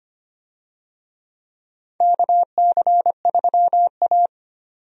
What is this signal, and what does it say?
Morse code, heard as a keyed tone, about 25 wpm, 700 Hz. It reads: KC3A
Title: USA Call Signs - 4 Characters - 25wpm